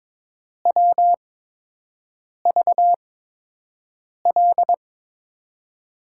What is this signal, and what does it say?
Morse code, heard as a keyed tone, about 22 wpm, spaced as for 6 wpm, 700 Hz. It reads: WVL
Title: USA Call Signs - 6 Characters - 8x Character Spacing - 22wpm